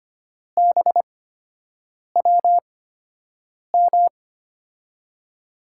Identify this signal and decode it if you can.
Morse code, heard as a keyed tone, about 25 wpm, spaced as for 7 wpm, 700 Hz. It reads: BWM
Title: USA Call Signs - 6 Characters - 8x Character Spacing - 25wpm